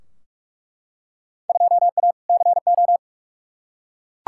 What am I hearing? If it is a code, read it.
Morse code, heard as a keyed tone, about 46 wpm, 700 Hz. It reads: 2A CO
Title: Field Day Exchanges - 45wpm